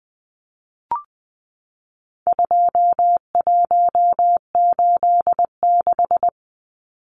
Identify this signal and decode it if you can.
Morse code, heard as a keyed tone, about 20 wpm, 700 Hz. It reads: 2186
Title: Numbers - 4-Digit 20wpm